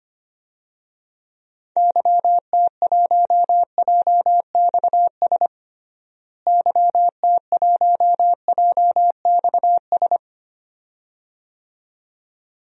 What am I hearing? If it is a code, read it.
Morse code, heard as a keyed tone, about 25 wpm, 700 Hz. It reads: YT1JXS YT1JXS
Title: DXCC-100 Call Signs - 6 Characters - Repeated 2x - 25wpm